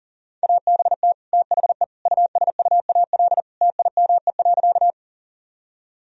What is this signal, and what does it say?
Morse code, heard as a keyed tone, about 40 wpm, 700 Hz. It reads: ABT THE USUAL TIME.
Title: Sentences from Top 400 Words and Abbreviations 40wpm